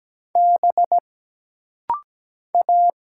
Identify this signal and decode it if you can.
Morse code, heard as a keyed tone, about 17 wpm, 700 Hz. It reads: B A
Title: Single Letter-Number Alt - Rapid-Fire 17wpm